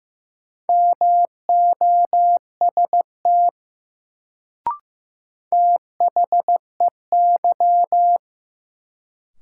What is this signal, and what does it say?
Morse code, heard as a keyed tone, about 15 wpm, 700 Hz. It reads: MOST THEY
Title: Sets of 2 Words - New Word Drill - 15wpm